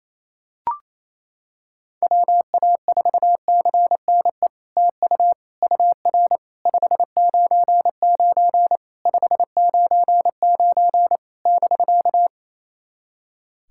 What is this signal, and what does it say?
Morse code, heard as a keyed tone, about 28 wpm, 700 Hz. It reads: WA4CNE TU UR 599 599 <BK>
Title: SOTA QSO Practice 28wpm